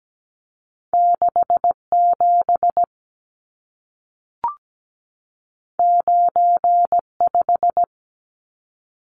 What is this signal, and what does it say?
Morse code, heard as a keyed tone, about 17 wpm, 700 Hz. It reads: 67 95
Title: Numbers - 2-Digit 17wpm